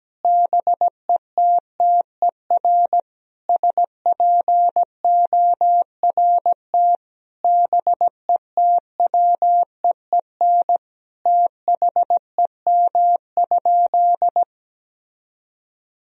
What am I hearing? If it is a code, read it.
Morse code, heard as a keyed tone, about 17 wpm, 700 Hz. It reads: BETTER SPORT BETWEEN THEM?
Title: Sentences from Top 1000 Words - Review Part 4 - 17wpm